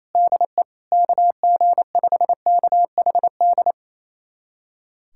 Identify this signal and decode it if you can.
Morse code, heard as a keyed tone, about 28 wpm, 700 Hz. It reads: DE KG5KHD
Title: USA Callsign DE Callsign 28wpm